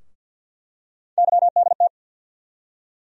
Morse code, heard as a keyed tone, about 47 wpm, 700 Hz. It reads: YDT
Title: USA Call Sign Postfixes - 3 Letters 50wpm